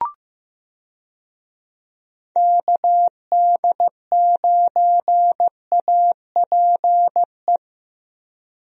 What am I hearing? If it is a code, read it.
Morse code, heard as a keyed tone, about 15 wpm, 700 Hz. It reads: KD9APE
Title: USA Call Signs - 6 Characters - Repeated 2x - 15wpm